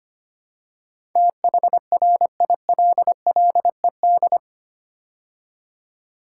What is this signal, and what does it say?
Morse code, heard as a keyed tone, about 25 wpm, 700 Hz. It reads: THRILLED
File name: 8 Letter Words - First 3 Letters Given - 25wpm